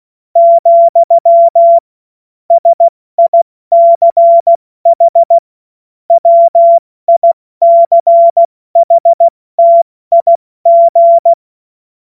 Morse code, heard as a keyed tone, about 16 wpm, spaced as for 14 wpm, 680 Hz. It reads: , SICH WICHTIG
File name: Morsekurs / Witze / Witz 24.Oktober 2023 70 F80